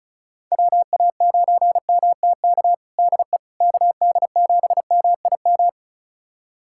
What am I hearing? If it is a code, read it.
Morse code, heard as a keyed tone, about 36 wpm, 690 Hz. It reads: WA9MTK DE KD7MIM
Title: USA Callsign DE Callsign - 6 Characters - 35wpm